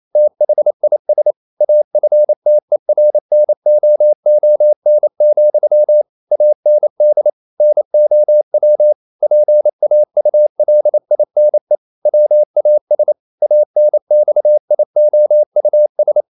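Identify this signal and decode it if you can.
Morse code, heard as a keyed tone, about 28 wpm, 600 Hz. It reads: THIS AFTERNOON, AND NOW PAULINE WAS ANXIOUS
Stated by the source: Morse code practice at 28 wpm, text of Ken Follett's novel Never in English, from CW med Gnister.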